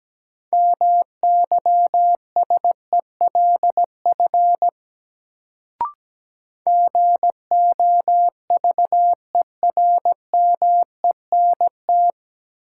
Morse code, heard as a keyed tone, about 17 wpm, 700 Hz. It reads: MYSELF GOVERMENT
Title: Sentences from Top 1300 Words - New Word Drill - 17wpm